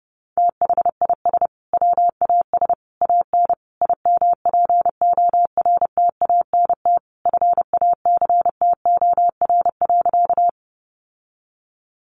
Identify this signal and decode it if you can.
Morse code, heard as a keyed tone, about 28 wpm, 700 Hz. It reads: THIS WAS AN IMPORTANT FACTOR.
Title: Sentences from Top 1300 Words 30wpm